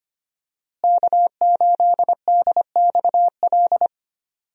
Morse code, heard as a keyed tone, about 25 wpm, 700 Hz. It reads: K8DXL
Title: USA Call Signs - 5 Characters - Repeated 3x - 25wpm